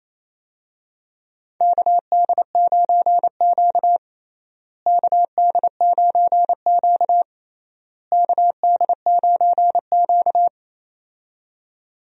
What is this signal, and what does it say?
Morse code, heard as a keyed tone, about 28 wpm, 700 Hz. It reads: KD9Q KD9Q KD9Q
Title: USA Call Signs - 4 Characters - Repeated 3x - 28wpm